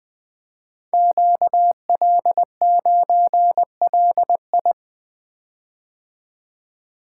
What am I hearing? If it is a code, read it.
Morse code, heard as a keyed tone, about 20 wpm, 700 Hz. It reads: QL9LI
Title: Code Groups - 5 Letter:Numbers - 20wpm